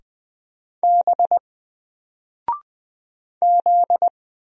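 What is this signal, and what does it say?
Morse code, heard as a keyed tone, about 20 wpm, 700 Hz. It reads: B Z
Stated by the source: Single letter and number Morse drill at 20 wpm, from N4LCD.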